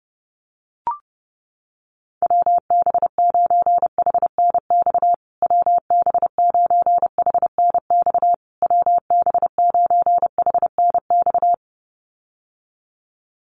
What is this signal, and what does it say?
Morse code, heard as a keyed tone, about 28 wpm, 700 Hz. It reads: WB9HNX WB9HNX WB9HNX
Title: USA Call Signs Repeated 3x 30wpm